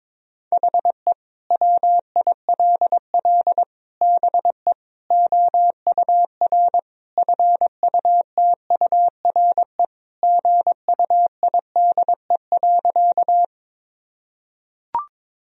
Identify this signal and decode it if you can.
Morse code, heard as a keyed tone, about 21 wpm, 700 Hz. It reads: HE WILL BE OUR FUTURE GUIDE.
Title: Sentences from Top 1000 Words - Review Part 3 - 22wpm